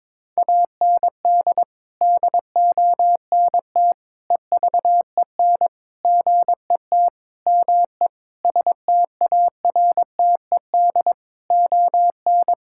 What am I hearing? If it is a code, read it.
Morse code, heard as a keyed tone, about 22 wpm, 700 Hz. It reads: AND DONT EVEN GET ME STARTED ON
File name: practice / 22wpm / Sets of 8 Words - 22wpm